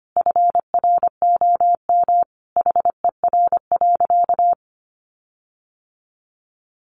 Morse code, heard as a keyed tone, about 25 wpm, 700 Hz. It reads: FROM HER.
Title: Sentences from Top 1300 Words 25wpm